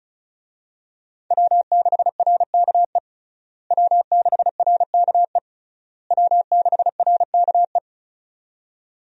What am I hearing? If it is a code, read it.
Morse code, heard as a keyed tone, about 36 wpm, 700 Hz. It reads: W6RKE W6RKE W6RKE
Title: USA Call Signs - 5 Characters - Repeated 3x - 35wpm